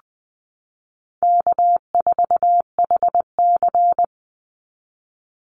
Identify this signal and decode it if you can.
Morse code, heard as a keyed tone, about 19 wpm, 700 Hz. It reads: K4HC
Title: Easy Call Signs Repeated 3x - 20wpm